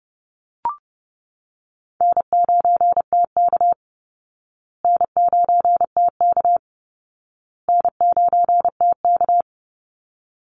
Morse code, heard as a keyed tone, about 30 wpm, 700 Hz. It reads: N9TK N9TK N9TK
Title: Easy Call Signs Repeated 3x - 30wpm